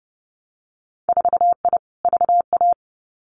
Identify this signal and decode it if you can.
Morse code, heard as a keyed tone, about 30 wpm, 700 Hz. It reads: 4I VA
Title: Winter Field Day Exchanges 30wpm